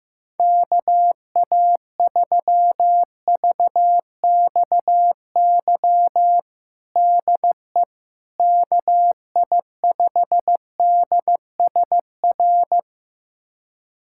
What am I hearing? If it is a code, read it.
Morse code, heard as a keyed tone, about 15 wpm, 700 Hz. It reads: KA3VXY DE KI5DSR
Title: USA Callsign DE Callsign 15wpm